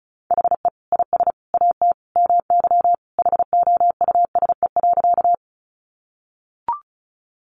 Morse code, heard as a keyed tone, about 34 wpm, 700 Hz. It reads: HE IS AT MY HOUSE.
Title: Sentences from Top 200 Words 35wpm